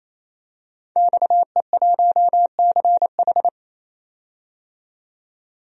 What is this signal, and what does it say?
Morse code, heard as a keyed tone, about 28 wpm, 700 Hz. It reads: XE1CH
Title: DXCC-100 Call Signs - 5 Characters - 28wpm